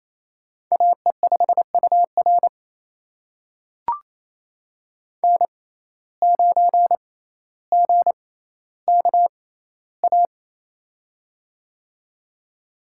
Morse code, heard as a keyed tone, about 28 wpm, spaced as for 14 wpm, 700 Hz. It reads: AE5UR N9GKA
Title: USA Call Signs - 5 Characters - 6x Character Spacing - 28wpm